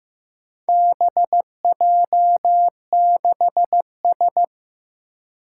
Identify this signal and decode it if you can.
Morse code, heard as a keyed tone, about 15 wpm, 700 Hz. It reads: BJ6S
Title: Four Challenging Letter-Number Combinations 15wpm